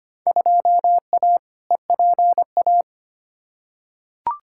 Morse code, heard as a keyed tone, about 25 wpm, 700 Hz. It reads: 2A EPA
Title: Field Day Exchanges - 25wpm